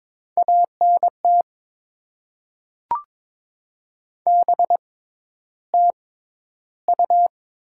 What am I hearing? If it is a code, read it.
Morse code, heard as a keyed tone, about 22 wpm, spaced as for 11 wpm, 700 Hz. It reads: ANT BTU
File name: Top 20 QSO Elements - 6x Character Spacing - 22wpm